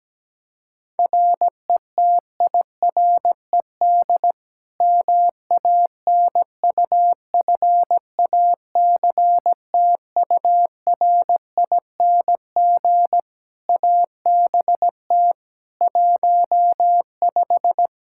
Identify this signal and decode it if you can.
Morse code, heard as a keyed tone, about 17 wpm, 700 Hz. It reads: RETIRED MANUFACTURING ABT 15
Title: Practice QSO - Exchange 3 - More Info - 17wpm